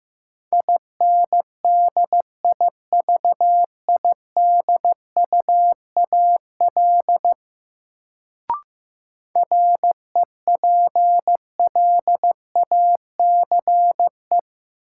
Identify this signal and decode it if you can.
Morse code, heard as a keyed tone, about 15 wpm, 700 Hz. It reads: INDIVIDUAL REPLACE
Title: Sentences from Top 1400 Words - New Word Drill Alt - 15wpm